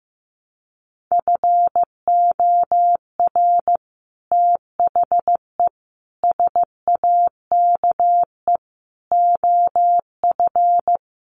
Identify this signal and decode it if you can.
Morse code, heard as a keyed tone, about 15 wpm, 700 Hz. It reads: FOR THE SAKE OF
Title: Sets of 5 Words Encore 15wpm